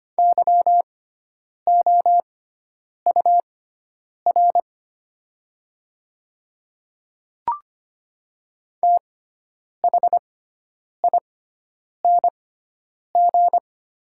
Morse code, heard as a keyed tone, about 25 wpm, spaced as for 9 wpm, 700 Hz. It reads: YOUR THING
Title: 6x Character Spacing - Top 300 Words - 25wpm